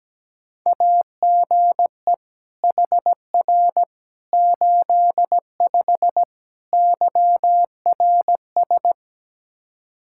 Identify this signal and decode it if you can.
Morse code, heard as a keyed tone, about 17 wpm, 700 Hz. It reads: AGE HR 85 YRS
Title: Practice QSO - Exchange 3 - More Info - 17wpm